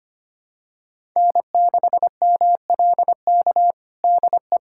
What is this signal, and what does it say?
Morse code, heard as a keyed tone, about 25 wpm, 700 Hz. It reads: N6MLK DE
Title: USA Callsign DE Callsign - 5 Characters - 25wpm